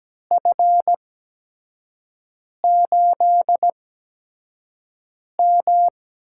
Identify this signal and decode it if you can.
Morse code, heard as a keyed tone, about 17 wpm, spaced as for 5 wpm, 700 Hz. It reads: F8M
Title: USA Call Signs - 8x Character Spacing - 17wpm